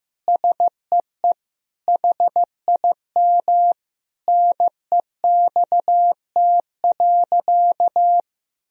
Morse code, heard as a keyed tone, about 16 wpm, 700 Hz. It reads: SEE HIM NEXT.
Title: Sentences from Top 1000 Words - Review Part 2 - 15wpm